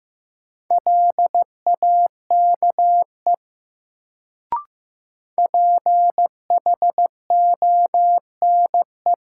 Morse code, heard as a keyed tone, about 15 wpm, 700 Hz. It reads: LAKE PHONE
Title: Sentences from Top 1100 Words - New Word Drill Alt - 15wpm